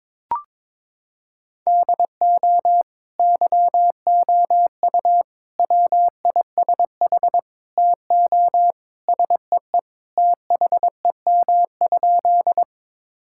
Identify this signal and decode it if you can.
Morse code, heard as a keyed tone, about 22 wpm, 700 Hz. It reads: DO YOU WISH TO SEE THEM?
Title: Sentences from Top 1600 Words 22wpm